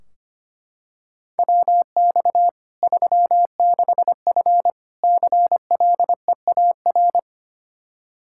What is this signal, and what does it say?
Morse code, heard as a keyed tone, about 25 wpm, 700 Hz. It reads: WX 36F CLEAR
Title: Practice QSO - Exchange 2 - Info - 25wpm